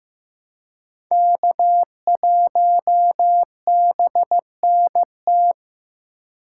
Morse code, heard as a keyed tone, about 15 wpm, 700 Hz. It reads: K1BNT
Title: USA Call Signs - 5 Characters - 4x Character Spacing - 15wpm